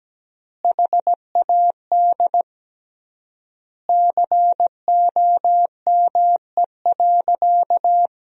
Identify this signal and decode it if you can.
Morse code, heard as a keyed tone, about 17 wpm, 700 Hz. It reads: HAD COME.
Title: 3x Word-Spacing - Sentences from Top 100 Words - 17wpm